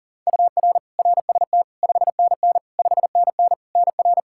Morse code, heard as a keyed tone, about 40 wpm, 700 Hz. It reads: UR RST 5NN 5NN NR
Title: Practice QSO - Exchange 1 - Beginning - 40wpm